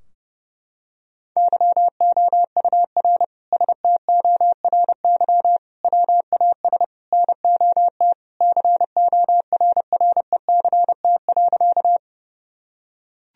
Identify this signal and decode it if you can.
Morse code, heard as a keyed tone, about 32 wpm, 700 Hz. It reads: YOUR STORY WAS NOT CORRECT.
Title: Sentences from Top 1000 Words - Review Part 2 - 30wpm